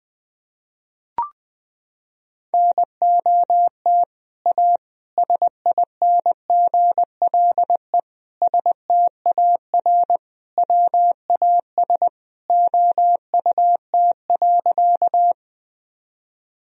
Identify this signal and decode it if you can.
Morse code, heard as a keyed tone, about 20 wpm, 700 Hz. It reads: NOT A SINGLE STAR WAS OUT.
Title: Sentences from Top 1000 Words - Review Part 4 - 20wpm